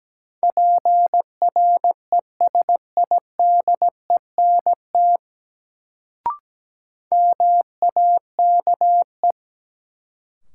Morse code, heard as a keyed tone, about 17 wpm, 700 Hz. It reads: PRESIDENT MAKE
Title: Sets of 2 Words - New Word Drill - 17wpm